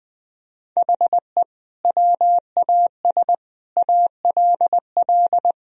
Morse code, heard as a keyed tone, about 20 wpm, 700 Hz. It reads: HE WAS ALL
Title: Sentences from Top 1000 Words - Review Part 3 - 20wpm